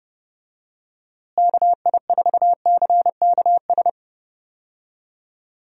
Morse code, heard as a keyed tone, about 30 wpm, 700 Hz. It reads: KI4CKS
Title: USA Call Signs - 6 Characters - Repeated 3x - 30wpm